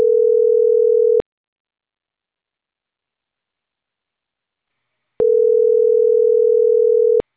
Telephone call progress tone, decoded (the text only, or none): ringback tone